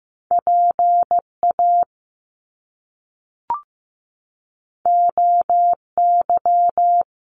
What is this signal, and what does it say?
Morse code, heard as a keyed tone, about 15 wpm, 700 Hz. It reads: PA OY
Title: Top Two Letter Combinations in Words 15wpm